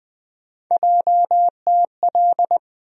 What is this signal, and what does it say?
Morse code, heard as a keyed tone, about 21 wpm, 700 Hz. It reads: JTL
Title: USA Call Sign Postfixes - 3 Letters 20wpm